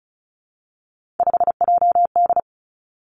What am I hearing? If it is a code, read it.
Morse code, heard as a keyed tone, about 36 wpm, 700 Hz. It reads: 5JD
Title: Three Challenging Letter-Number Combinations 35wpm